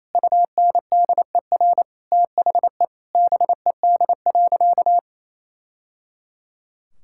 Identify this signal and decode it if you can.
Morse code, heard as a keyed tone, about 28 wpm, 700 Hz. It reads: UNDER THE BED.
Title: Sentences from Top 1000 Words - Review Part 5 - 28wpm